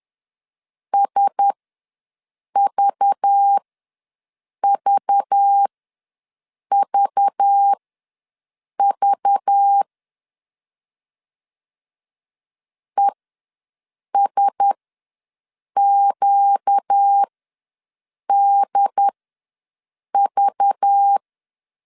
Morse code, heard as a keyed tone, about 10 wpm, spaced as for 6 wpm, 800 Hz. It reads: SVVVV ESQDV
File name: LEZIONI / 5 LBVC / 5 LBV ws 10wpm 3645